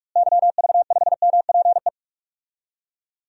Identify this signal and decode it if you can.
Morse code, heard as a keyed tone, about 41 wpm, 700 Hz. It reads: YV5MPE
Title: DXCC-100 Call Signs - 6 Characters - Repeated 3x - 45wpm